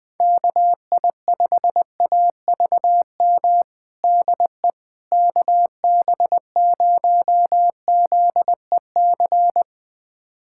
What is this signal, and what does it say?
Morse code, heard as a keyed tone, about 20 wpm, 700 Hz. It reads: KI5AVM DE KB0ZEC
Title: USA Callsign DE Callsign - 6 Characters - 20wpm